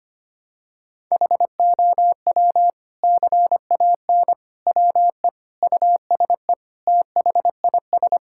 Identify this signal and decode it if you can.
Morse code, heard as a keyed tone, about 25 wpm, 700 Hz. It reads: HOW CAN WE USE THIS
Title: Sentences from Top 1000 Words - Review Part 5 - 25wpm